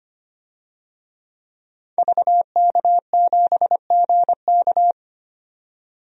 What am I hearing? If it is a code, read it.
Morse code, heard as a keyed tone, about 25 wpm, 700 Hz. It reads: VK7GK
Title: DXCC-100 Call Signs - 5 Characters - 25wpm